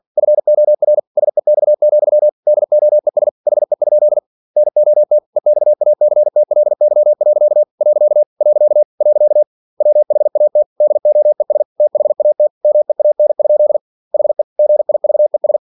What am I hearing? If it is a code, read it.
Morse code, heard as a keyed tone, about 48 wpm, 600 Hz. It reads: FOR SEX, DOES HE? NOT EXACTLY. . WHAT DOES THAT MEAN? HE GIVES
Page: Never 47wpm